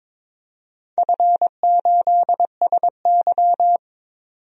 Morse code, heard as a keyed tone, about 22 wpm, 700 Hz. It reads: F8SY
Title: DXCC-100 Call Signs - 4 Characters - 22wpm